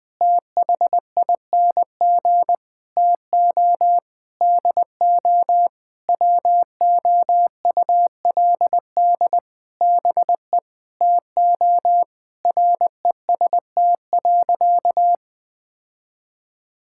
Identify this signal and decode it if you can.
Morse code, heard as a keyed tone, about 20 wpm, 700 Hz. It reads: THING TO DO WOULD BE TO REST.
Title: Sentences with 8-Word Phrases - 20wpm